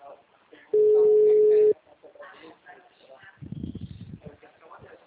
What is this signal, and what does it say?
Ringback tone (Europe, 425 Hz), with speech.